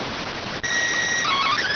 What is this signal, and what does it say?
SSTV (slow-scan television), PD 290.